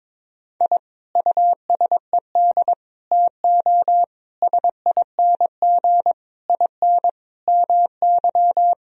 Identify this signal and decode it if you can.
Morse code, heard as a keyed tone, about 22 wpm, 700 Hz. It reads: I USED TO SING IN MY
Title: Sentences from Top 1000 Words - Review Part 2 - 22wpm